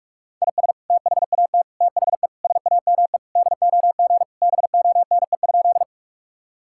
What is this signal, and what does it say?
Morse code, heard as a keyed tone, about 46 wpm, 700 Hz. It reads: IS THAT THE SAME DOG BONE?
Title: Sentences from Top 1600 Words 45wpm